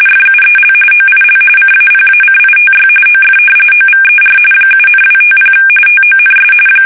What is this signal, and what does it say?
RTTY, about 45.45 baud. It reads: FUEL OIL CAPACITY OF OVER 7,250 TONS
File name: fsk850 / battleshipnj